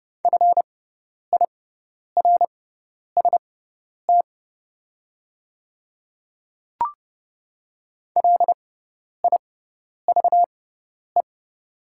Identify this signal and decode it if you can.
Morse code, heard as a keyed tone, about 30 wpm, spaced as for 10 wpm, 700 Hz. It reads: FIRST LIVE
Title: 6x Character Spacing - Top 200 Words - 30wpm